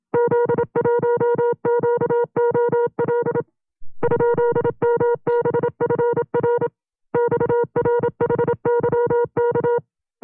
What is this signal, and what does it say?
Morse code, heard as a keyed tone, about 27 wpm, 480 Hz. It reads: Z1QOL ?MBFR XRHYK
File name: CW-practice/25WPM